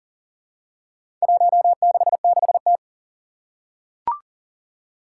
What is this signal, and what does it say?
Morse code, heard as a keyed tone, about 41 wpm, 700 Hz. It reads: 166T
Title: Four Challenging Letter-Number Combinations 40wpm